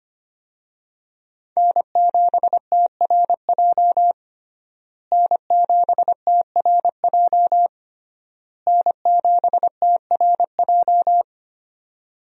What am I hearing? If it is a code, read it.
Morse code, heard as a keyed tone, about 25 wpm, 700 Hz. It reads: N7TRJ N7TRJ N7TRJ
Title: USA Call Signs - 5 Characters - Repeated 3x - 25wpm